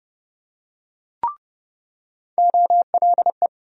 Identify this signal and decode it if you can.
Morse code, heard as a keyed tone, about 30 wpm, 700 Hz. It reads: OLE
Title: USA Call Sign Postfixes - 3 Letters 30wpm